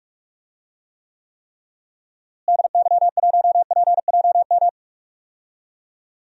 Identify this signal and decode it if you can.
Morse code, heard as a keyed tone, about 46 wpm, 700 Hz. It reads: DY1PJM
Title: DXCC-100 Call Signs - 6 Characters - Repeated 3x - 45wpm